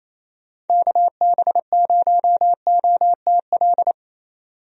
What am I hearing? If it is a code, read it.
Morse code, heard as a keyed tone, about 27 wpm, 700 Hz. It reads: KB0OTL
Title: USA Call Signs - 6 Characters - 28wpm